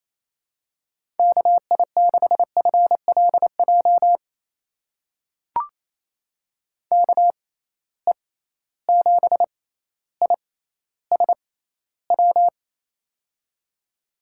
Morse code, heard as a keyed tone, about 27 wpm, spaced as for 14 wpm, 700 Hz. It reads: KI6FLJ KE7ISW
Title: USA Call Signs - 6x Character Spacing - 28wpm